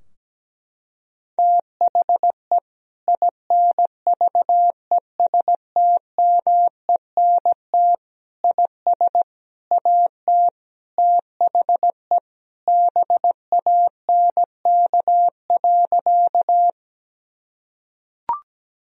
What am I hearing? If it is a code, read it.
Morse code, heard as a keyed tone, about 17 wpm, 700 Hz. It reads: THE INVESTMENT IS AT THE BANK.
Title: Sentences from Top 1000 Words - Review Part 4 - 17wpm